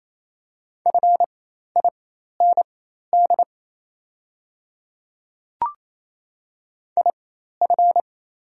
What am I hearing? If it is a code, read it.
Morse code, heard as a keyed tone, about 27 wpm, spaced as for 13 wpm, 700 Hz. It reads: FIND IF
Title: 4x Character Spacing - Top 100 Words - 28wpm